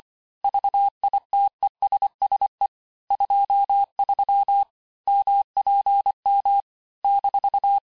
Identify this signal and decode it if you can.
Morse code, heard as a keyed tone, about 25 wpm, 800 Hz. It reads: VITESSE 23 MPM -
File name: texte-chiffre-vitesse-23mpm